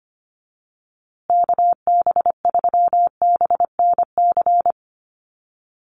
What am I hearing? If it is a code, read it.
Morse code, heard as a keyed tone, about 25 wpm, 700 Hz. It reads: KB3BNC
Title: USA Call Signs Repeated 3x 25wpm